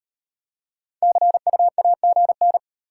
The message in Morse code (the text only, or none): CU AGN